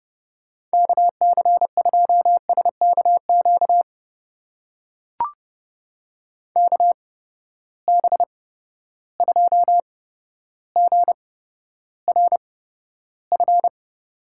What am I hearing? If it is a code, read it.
Morse code, heard as a keyed tone, about 30 wpm, spaced as for 13 wpm, 700 Hz. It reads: KC2SKQ KB2GRF
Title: USA Call Signs - 6 Characters - 8x Character Spacing - 30wpm